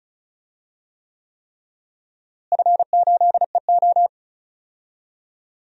Morse code, heard as a keyed tone, about 36 wpm, 700 Hz. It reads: F8EO
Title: DXCC-100 Call Signs - 4 Characters - Repeated 3x - 35wpm